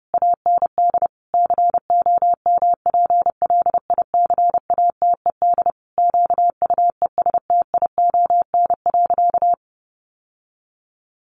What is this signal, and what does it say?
Morse code, heard as a keyed tone, about 30 wpm, 700 Hz. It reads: AND COMPLICATED QUESTION.
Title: Sentences from Top 1200 Words 30wpm